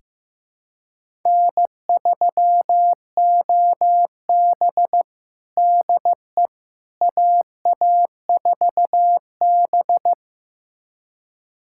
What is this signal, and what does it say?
Morse code, heard as a keyed tone, about 15 wpm, 700 Hz. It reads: N3OB DE AA4B
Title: USA Callsign DE Callsign - 4 Characters - 15wpm